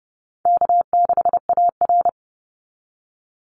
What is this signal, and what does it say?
Morse code, heard as a keyed tone, about 30 wpm, 700 Hz. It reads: K6AR
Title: Call Signs Repeated 2x - 30wpm